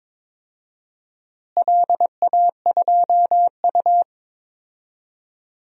Morse code, heard as a keyed tone, about 22 wpm, 700 Hz. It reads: LA2U
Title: DXCC-100 Call Signs - 4 Characters - Repeated 2x - 22wpm